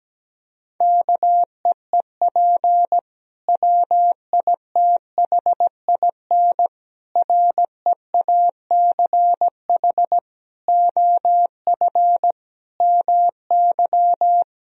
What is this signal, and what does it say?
Morse code, heard as a keyed tone, about 17 wpm, 700 Hz. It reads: KEEP WITHIN REACH OF MY
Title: Sentences from Top 1000 Words - Review Part 4 - 17wpm